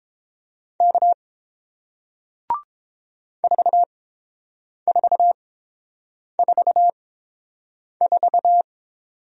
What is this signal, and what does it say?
Morse code, heard as a keyed tone, about 30 wpm, spaced as for 7 wpm, 700 Hz. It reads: K 4444